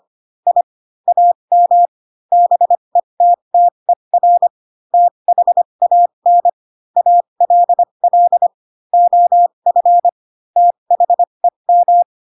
Morse code, heard as a keyed tone, about 25 wpm, spaced as for 22 wpm, 700 Hz. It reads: I AM BETTER THAN ALL OF THEM